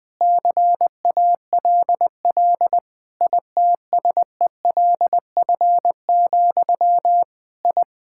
Morse code, heard as a keyed tone, about 20 wpm, 700 Hz. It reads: CALL ITSELF, I
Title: Sentences from Top 1000 Words - Review Part 5 - 20wpm